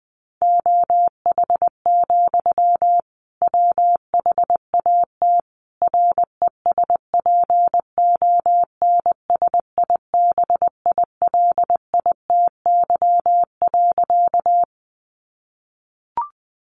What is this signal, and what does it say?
Morse code, heard as a keyed tone, about 20 wpm, 700 Hz. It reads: OH, WHAT RESPONSIBILITY.
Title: Sentences from Top 1200 Words 20wpm